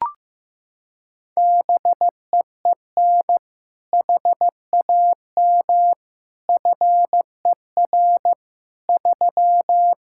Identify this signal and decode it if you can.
Morse code, heard as a keyed tone, about 15 wpm, 700 Hz. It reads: BEEN HAM FER 3
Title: K1USN SST QSO Practice 15wpm